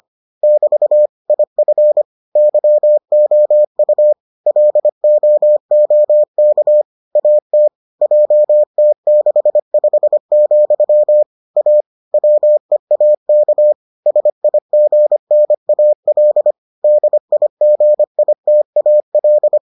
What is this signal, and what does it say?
Morse code, heard as a keyed tone, about 25 wpm, 600 Hz. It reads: <BT> IF YOU LOOK AT JT65, A WEAK SIGNAL DIGITAL